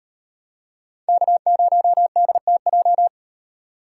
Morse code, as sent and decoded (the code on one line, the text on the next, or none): -.- ----- -.. - .---
K0DTJ